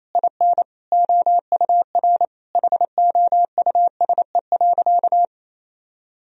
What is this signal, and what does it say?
Morse code, heard as a keyed tone, about 28 wpm, 700 Hz. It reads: IN OUR HOUSE.
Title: Sentences from Top 1000 Words - Review Part 1 - 28wpm